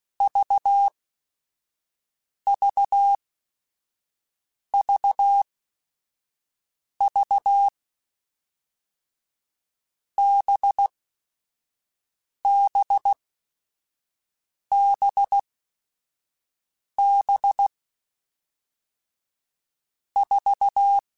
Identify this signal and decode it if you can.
Morse code, heard as a keyed tone, about 16 wpm, 800 Hz. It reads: V V V V B B B B 4
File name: Groups@5 / Groups@5 mp3s / VB46@5